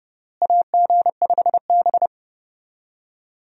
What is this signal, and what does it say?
Morse code, heard as a keyed tone, about 32 wpm, 700 Hz. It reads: AG5B
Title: USA Call Signs - 4 Characters - Repeated 2x - 30wpm